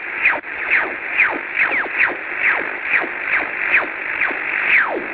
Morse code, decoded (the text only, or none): none